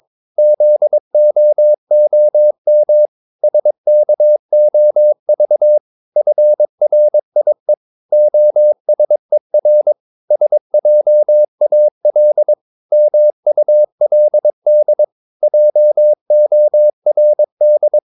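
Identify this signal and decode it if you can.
Morse code, heard as a keyed tone, about 22 wpm, 600 Hz. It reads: ZOOM SKOV FRIE OSER SJAL MULD JORD
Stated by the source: Morse practice of short Danish words at 22 wpm, from CW med Gnister.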